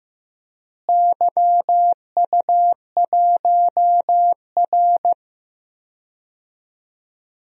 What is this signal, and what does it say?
Morse code, heard as a keyed tone, about 15 wpm, 700 Hz. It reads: YU1R
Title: DXCC-100 Call Signs - 4 Characters - Repeated 2x - 15wpm